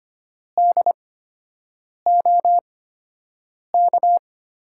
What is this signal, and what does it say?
Morse code, heard as a keyed tone, about 25 wpm, spaced as for 7 wpm, 700 Hz. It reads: DOK
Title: USA Call Signs - 6 Characters - 8x Character Spacing - 25wpm